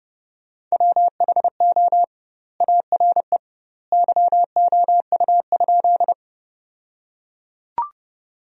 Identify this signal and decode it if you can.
Morse code, heard as a keyed tone, about 30 wpm, 700 Hz. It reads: WHO ARE YOU?
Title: 2x Word-Spacing - Sentences from Top 100 Words - 30wpm